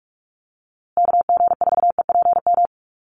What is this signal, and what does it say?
Morse code, heard as a keyed tone, about 44 wpm, 700 Hz. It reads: KG4EPM
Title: USA Call Signs Repeated 2x 45wpm